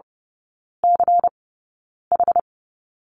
Morse code, heard as a keyed tone, about 28 wpm, spaced as for 9 wpm, 690 Hz. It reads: CH
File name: Lesson 16e - Letter c - Three Characters 30wpm